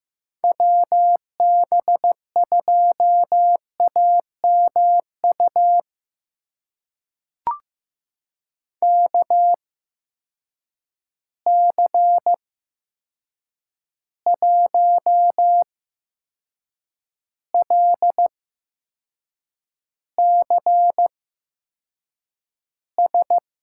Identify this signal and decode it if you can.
Morse code, heard as a keyed tone, about 15 wpm, spaced as for 6 wpm, 700 Hz. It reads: WB2AMU KC1LCS
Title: USA Call Signs - 6 Characters - 8x Character Spacing - 15wpm